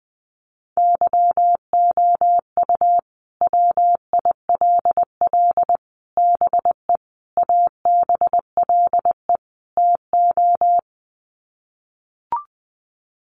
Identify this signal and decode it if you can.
Morse code, heard as a keyed tone, about 20 wpm, 700 Hz. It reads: YOU WILL BE ABLE TO
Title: Sets of 5 Words Encore2 20wpm